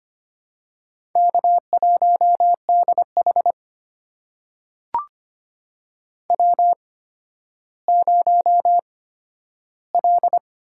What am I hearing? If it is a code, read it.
Morse code, heard as a keyed tone, about 25 wpm, spaced as for 12 wpm, 700 Hz. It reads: K1DH W0L
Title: USA Call Signs - 4 Characters - 8x Character Spacing - 25wpm